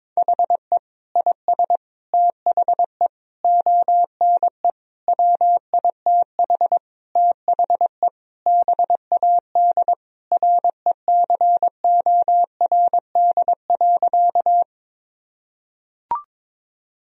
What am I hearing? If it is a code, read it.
Morse code, heard as a keyed tone, about 22 wpm, 700 Hz. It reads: HE IS THE ONE WITH THE BAD RECORD.
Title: Sentences from Top 1000 Words - Review Part 3 - 22wpm